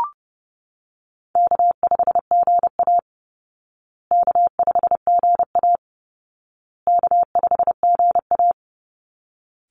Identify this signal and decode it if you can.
Morse code, heard as a keyed tone, about 30 wpm, 700 Hz. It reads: K5GA K5GA K5GA
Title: Easy Call Signs Repeated 3x - 30wpm